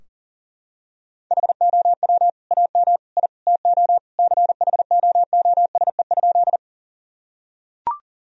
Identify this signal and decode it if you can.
Morse code, heard as a keyed tone, about 37 wpm, 700 Hz. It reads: HOW AM I TO CHOOSE?
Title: Sentences from Top 1000 Words - Review Part 4 - 40wpm